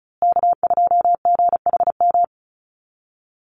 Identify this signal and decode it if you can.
Morse code, heard as a keyed tone, about 34 wpm, 700 Hz. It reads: K2GHM
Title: USA Call Signs Repeated 3x 35wpm